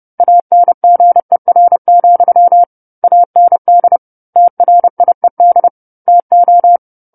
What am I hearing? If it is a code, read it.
Morse code, heard as a keyed tone, about 30 wpm, 700 Hz. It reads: ANGER, AND TRIED TO
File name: 10-Chapter-10 30 WPM